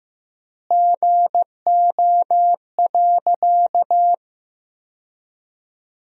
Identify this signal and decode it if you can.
Morse code, heard as a keyed tone, about 15 wpm, 700 Hz. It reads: GO.
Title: 3x Word-Spacing - Sentences from Top 400 Words - 15wpm